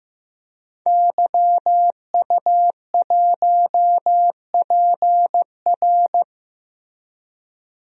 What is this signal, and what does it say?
Morse code, heard as a keyed tone, about 16 wpm, 700 Hz. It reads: YU1PR
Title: DXCC-100 Call Signs - 5 Characters - Repeated 2x - 15wpm